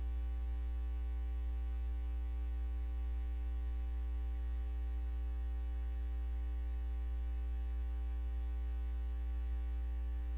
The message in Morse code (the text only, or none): none